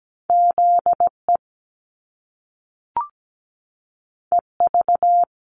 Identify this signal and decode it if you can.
Morse code, heard as a keyed tone, about 17 wpm, 700 Hz. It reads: ZE EV
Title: Top Two Letter Combinations in Words 17wpm